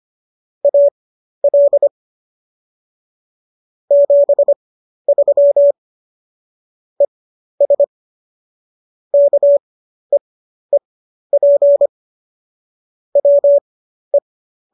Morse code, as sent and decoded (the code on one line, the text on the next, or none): .- .-.. / --... ...-- / . ... / -.- . .--. / .-- .
AL 73 ES KEEP WE